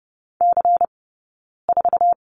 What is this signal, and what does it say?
Morse code, heard as a keyed tone, about 30 wpm, 700 Hz. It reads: C 4
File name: Lesson 16d - Letter c - Two Characters 30wpm